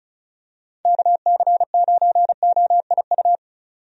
Morse code, heard as a keyed tone, about 34 wpm, 700 Hz. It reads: KC9OIU
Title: USA Call Signs - 6 Characters - 35wpm